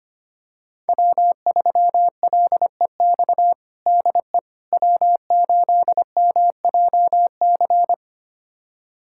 Morse code, heard as a keyed tone, about 25 wpm, 700 Hz. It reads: W3LEX DE W8MJC
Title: USA Callsign DE Callsign - 5 Characters - 25wpm